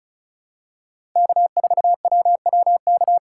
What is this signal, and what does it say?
Morse code, heard as a keyed tone, about 34 wpm, 700 Hz. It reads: K4WWK
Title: USA Call Signs - 5 Characters - 4x Character Spacing - 35wpm